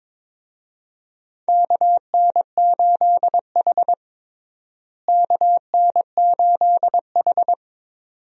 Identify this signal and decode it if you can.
Morse code, heard as a keyed tone, about 22 wpm, 700 Hz. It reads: KN8H KN8H
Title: USA Call Signs - 4 Characters - Repeated 2x - 22wpm